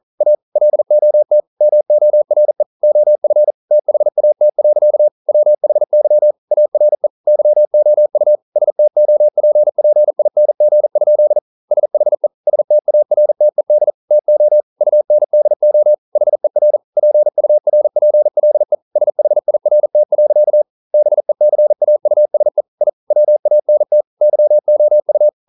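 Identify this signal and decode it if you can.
Morse code, heard as a keyed tone, about 40 wpm, 600 Hz. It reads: A LOT MORE OF THAT. WHY ARE YOU STOPPING? SHE STARTED TO UNDO HER PURPLE SHIRT. BECAUSE I WANT YOU
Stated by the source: Morse code practice at 41 wpm, from CW med Gnister.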